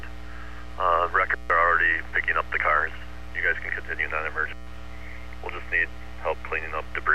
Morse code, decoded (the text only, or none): none